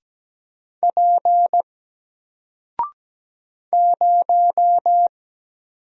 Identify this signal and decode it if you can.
Morse code, heard as a keyed tone, about 17 wpm, 700 Hz. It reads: P 0
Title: Single Character (Letter:Number:Punctuation:Prosign) Alt 17wpm